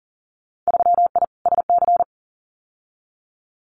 Morse code, heard as a keyed tone, about 42 wpm, 700 Hz. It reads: 3I SC
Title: Winter Field Day Exchanges Encore 40wpm